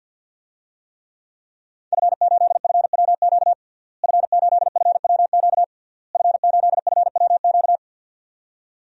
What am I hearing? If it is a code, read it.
Morse code, heard as a keyed tone, about 49 wpm, 700 Hz. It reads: F8FWQ F8FWQ F8FWQ
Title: DXCC-100 Call Signs - 5 Characters - Repeated 3x - 50wpm